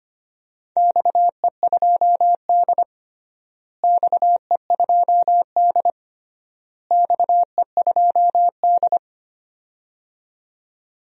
Morse code, heard as a keyed tone, about 25 wpm, 700 Hz. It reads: XE2D XE2D XE2D
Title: DXCC-100 Call Signs - 4 Characters - Repeated 3x - 25wpm